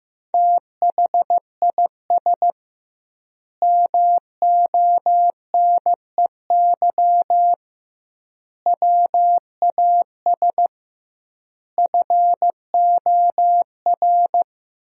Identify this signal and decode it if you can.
Morse code, heard as a keyed tone, about 15 wpm, 700 Hz. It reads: THIS MONEY WAS FOR
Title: 2x Word-Spacing - Sentences from Top 400 Words - 15wpm